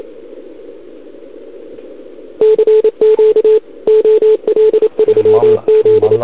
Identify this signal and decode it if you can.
Morse code, heard as a keyed tone, about 29 wpm, 440 Hz. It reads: CQ OL3Z
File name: CQWWCW2021 / OL3Z